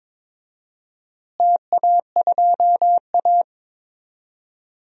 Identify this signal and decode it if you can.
Morse code, heard as a keyed tone, about 21 wpm, 700 Hz. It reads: TA2A
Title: DXCC-100 Call Signs - 4 Characters - Repeated 2x - 22wpm